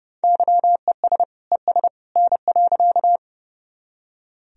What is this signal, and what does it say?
Morse code, heard as a keyed tone, about 30 wpm, 700 Hz. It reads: YES ES N.
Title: Sentences from Top 500 Words and Abbreviations 30wpm